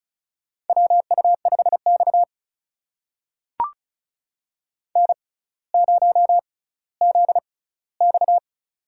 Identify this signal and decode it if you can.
Morse code, heard as a keyed tone, about 36 wpm, spaced as for 18 wpm, 700 Hz. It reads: WU5X N0ZX
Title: USA Call Signs - 4 Characters - 6x Character Spacing - 35wpm